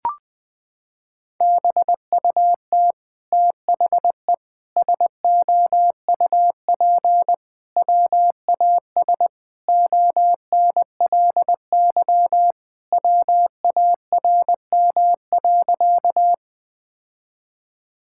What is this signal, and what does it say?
Morse code, heard as a keyed tone, about 20 wpm, 700 Hz. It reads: BUT THE SOUP WAS ONLY WARM.
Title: Sentences from Top 1000 Words - Review Part 6 - 20wpm